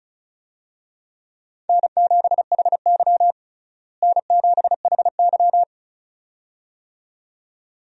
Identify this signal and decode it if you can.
Morse code, heard as a keyed tone, about 36 wpm, 700 Hz. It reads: N7HY N7HY
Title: USA Call Signs - 4 Characters - Repeated 2x - 35wpm